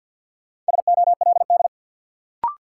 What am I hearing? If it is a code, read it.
Morse code, heard as a keyed tone, about 52 wpm, 700 Hz. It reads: SOLD
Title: Sentences from Top 1400 Words - New Word Drill Alt - 50wpm